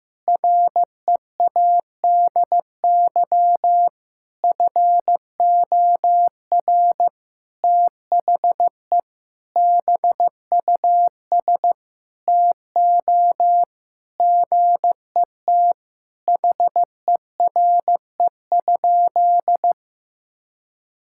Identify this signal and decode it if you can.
Morse code, heard as a keyed tone, about 15 wpm, 700 Hz. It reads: READY FOR THE BUS TO GET HERE?
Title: Sentences from Top 1000 Words - Review Part 6 - 15wpm